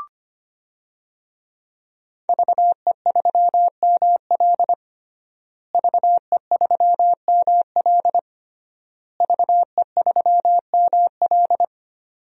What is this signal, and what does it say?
Morse code, heard as a keyed tone, about 25 wpm, 700 Hz. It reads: VE3ML VE3ML VE3ML
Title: DXCC-100 Call Signs - 5 Characters - Repeated 3x - 25wpm